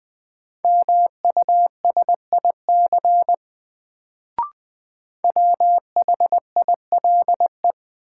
Morse code, heard as a keyed tone, about 20 wpm, 700 Hz. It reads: MUSIC WHILE
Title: Sentences from Top 300 Words - New Word Drill - 20wpm